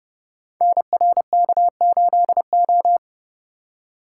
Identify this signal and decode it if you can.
Morse code, heard as a keyed tone, about 30 wpm, 700 Hz. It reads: NRK8O
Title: Code Groups - 5 Letter:Numbers - 30wpm